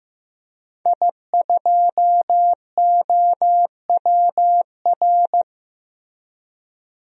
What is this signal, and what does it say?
Morse code, heard as a keyed tone, about 15 wpm, 700 Hz. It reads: I2OWR
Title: DXCC-100 Call Signs - 5 Characters - Repeated 2x - 15wpm